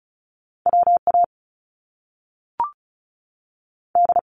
Morse code, heard as a keyed tone, about 35 wpm, 700 Hz. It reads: WA D
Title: US States 35wpm